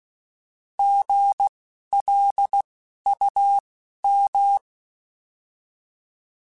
Morse code, heard as a keyed tone, about 16 wpm, spaced as for 11 wpm, 800 Hz. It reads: GLUM
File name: Alphabetical / Alphabetical mp3s / Gs @10